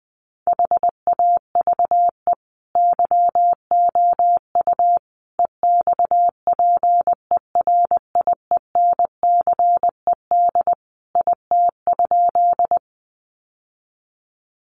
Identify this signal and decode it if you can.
Morse code, heard as a keyed tone, about 20 wpm, 700 Hz. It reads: HAVE YOU EXPERIENCED IT?
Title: Sentences from Top 1100 Words 20wpm